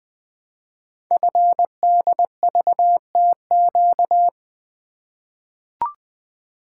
Morse code, heard as a keyed tone, about 20 wpm, 700 Hz. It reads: FDVTQ
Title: Code Groups - 5 Letters - 20wpm